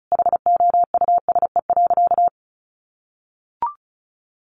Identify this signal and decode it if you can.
Morse code, heard as a keyed tone, about 34 wpm, 700 Hz. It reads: HOUSE.
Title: Sentences from Top 1400 Words 35wpm